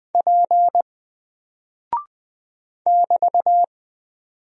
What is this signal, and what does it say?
Morse code, heard as a keyed tone, about 20 wpm, 700 Hz. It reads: P <BT>
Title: Single Character (Letter:Number:Punctuation:Prosign) Alt 20wpm